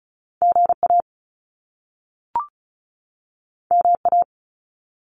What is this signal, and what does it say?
Morse code, heard as a keyed tone, about 36 wpm, 700 Hz. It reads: GA MA
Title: US States 35wpm